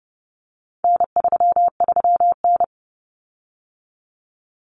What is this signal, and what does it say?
Morse code, heard as a keyed tone, about 30 wpm, 700 Hz. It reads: N33N
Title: Common Cut Numbers - 4-Digit 30wpm